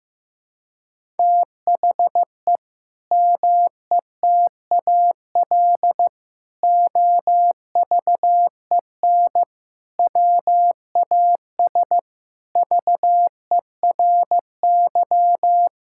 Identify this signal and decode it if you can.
Morse code, heard as a keyed tone, about 15 wpm, 700 Hz. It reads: THE METAL OVEN WAS VERY
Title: Sentences from Top 1000 Words - Review Part 6 - 15wpm